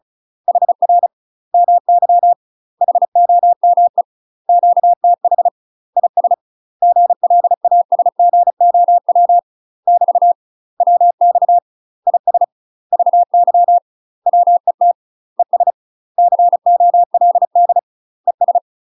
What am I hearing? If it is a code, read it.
Morse code, heard as a keyed tone, about 36 wpm, 700 Hz. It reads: HR MY HOME QTH IS GLASGOW <BT> WX IS VY WET ES COLD ES
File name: UK MASTER QSOs TRAINING / 35WPM / UK MASTER QSOs 0028